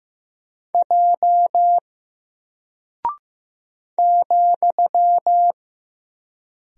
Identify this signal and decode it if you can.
Morse code, heard as a keyed tone, about 15 wpm, 700 Hz. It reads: J ,
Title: Single Character (Letter:Number:Punctuation:Prosign) Alt 15wpm